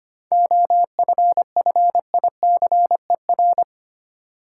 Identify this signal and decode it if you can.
Morse code, heard as a keyed tone, about 25 wpm, 700 Hz. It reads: OFFICER
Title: Seven Letter Words - 25wpm